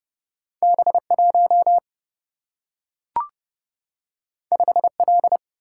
Morse code, heard as a keyed tone, about 30 wpm, 700 Hz. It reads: B 1 5 L
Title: Single-Letter Single-Number 30wpm